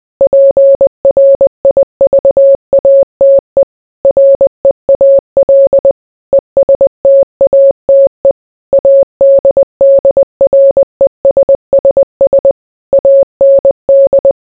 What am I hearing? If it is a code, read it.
Morse code, heard as a keyed tone, about 20 wpm, 550 Hz. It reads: PRIVATE REAL ESTATE ADDRESSS AND